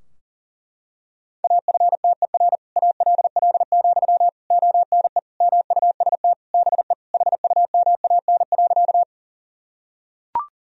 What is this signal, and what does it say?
Morse code, heard as a keyed tone, about 40 wpm, 700 Hz. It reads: AFTER ALL, ONE MUST BE HUMAN.
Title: Sentences from Top 1000 Words - Review Part 3 - 40wpm